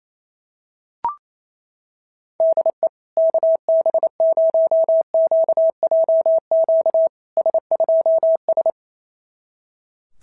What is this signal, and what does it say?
Morse code, heard as a keyed tone, about 28 wpm, 650 Hz. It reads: DE KB0QJQ S2S
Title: SOTA QSO Practice 28wpm